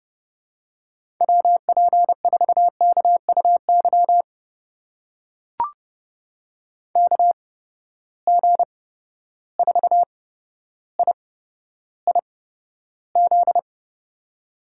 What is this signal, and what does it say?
Morse code, heard as a keyed tone, about 30 wpm, spaced as for 13 wpm, 700 Hz. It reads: WP4KUY KG4IIZ
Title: USA Call Signs - 6 Characters - 8x Character Spacing - 30wpm